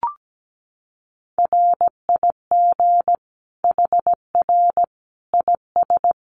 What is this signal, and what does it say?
Morse code, heard as a keyed tone, about 17 wpm, 700 Hz. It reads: RIG HR IS
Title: Common QSO Phrases 17wpm